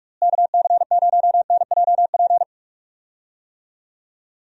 Morse code, heard as a keyed tone, about 41 wpm, 700 Hz. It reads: KC0NJP
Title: USA Call Signs - 6 Characters - Repeated 2x - 45wpm